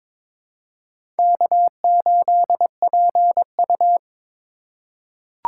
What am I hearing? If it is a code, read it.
Morse code, heard as a keyed tone, about 22 wpm, 700 Hz. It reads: K8PU
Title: USA Call Signs - 4x Character Spacing - 22wpm